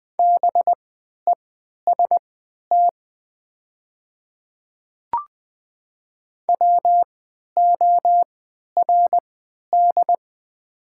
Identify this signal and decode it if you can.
Morse code, heard as a keyed tone, about 20 wpm, spaced as for 11 wpm, 700 Hz. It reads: BEST WORD
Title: 3x Character Spacing - Top 500 Words - 20wpm